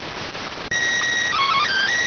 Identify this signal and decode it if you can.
SSTV (slow-scan television), PD 290.